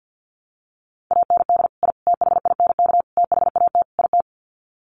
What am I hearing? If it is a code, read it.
Morse code, heard as a keyed tone, about 50 wpm, 700 Hz. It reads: AND I THINK THAT IT
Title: Sets of 5 Words Encore2 50wpm